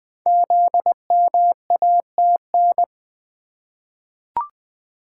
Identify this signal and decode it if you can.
Morse code, heard as a keyed tone, about 20 wpm, 700 Hz. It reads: ZMATN
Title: Code Groups - 5 Letters - 20wpm